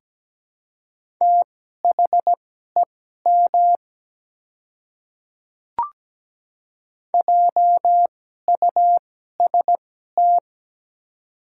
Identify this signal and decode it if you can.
Morse code, heard as a keyed tone, about 17 wpm, spaced as for 12 wpm, 700 Hz. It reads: THEM JUST
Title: 2x Character Spacing - Top 200 Words - 17wpm